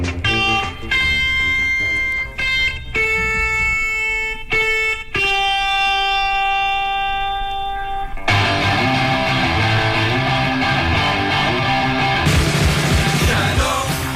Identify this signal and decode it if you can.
Dial-up modem handshake.